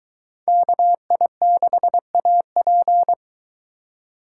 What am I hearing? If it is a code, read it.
Morse code, heard as a keyed tone, about 22 wpm, 700 Hz. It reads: KI6AP